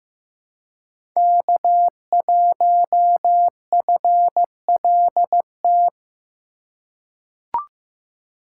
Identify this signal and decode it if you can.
Morse code, heard as a keyed tone, about 16 wpm, 700 Hz. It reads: K1FLT
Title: USA Call Signs - 5 Characters - 6x Character Spacing - 15wpm